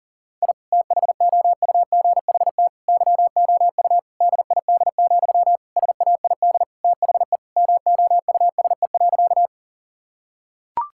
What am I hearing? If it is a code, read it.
Morse code, heard as a keyed tone, about 37 wpm, 700 Hz. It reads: I THOUGHT YOU DID, SAID THE MOUSE.
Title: Sentences from Top 1000 Words - Review Part 5 - 40wpm